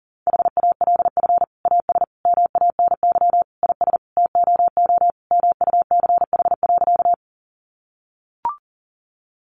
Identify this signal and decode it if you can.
Morse code, heard as a keyed tone, about 40 wpm, 700 Hz. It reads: HALF AS MANY IS TOO MUCH.
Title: Sentences from Top 400 Words 40wpm